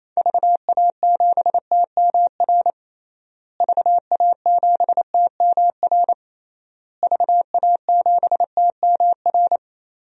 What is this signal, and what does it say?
Morse code, heard as a keyed tone, about 28 wpm, 700 Hz. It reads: VA7TMR VA7TMR VA7TMR
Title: DXCC-100 Call Signs - 6 Characters - Repeated 3x - 28wpm